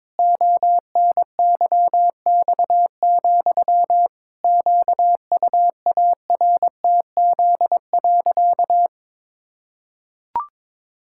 Morse code, heard as a keyed tone, about 22 wpm, 700 Hz. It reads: ONYX, QUARTZ.
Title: Pangram Challenge v2 - 22wpm